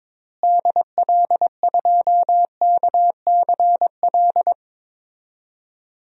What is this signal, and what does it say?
Morse code, heard as a keyed tone, about 22 wpm, 700 Hz. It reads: DL2KCL
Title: DXCC-100 Call Signs - 6 Characters - Repeated 3x - 22wpm